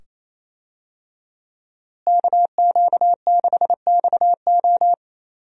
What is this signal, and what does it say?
Morse code, heard as a keyed tone, about 28 wpm, 700 Hz. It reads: KQ6XO
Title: USA Call Signs - 5 Characters - Repeated 2x - 28wpm